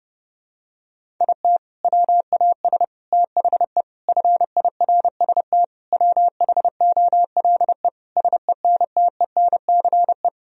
Morse code, heard as a keyed tone, about 30 wpm, 700 Hz. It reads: IT WAS THE FIRST WHOLE SENTENCE
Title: Sentences from Top 1000 Words - Review Part 1 - 30wpm